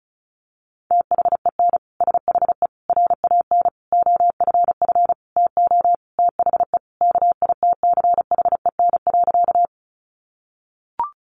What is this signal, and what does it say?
Morse code, heard as a keyed tone, about 34 wpm, 700 Hz. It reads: THEN SHE RAN OFF TO THE KITCHEN.
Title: Sentences from Top 1400 Words 35wpm